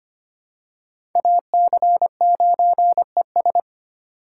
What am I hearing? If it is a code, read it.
Morse code, heard as a keyed tone, about 25 wpm, 700 Hz. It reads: AC9ES
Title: USA Call Signs - 5 Characters - 3x Character Spacing - 25wpm